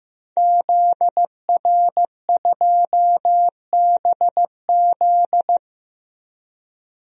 Morse code, heard as a keyed tone, about 15 wpm, 700 Hz. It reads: ZR2BZ
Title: DXCC-100 Call Signs - 5 Characters - 15wpm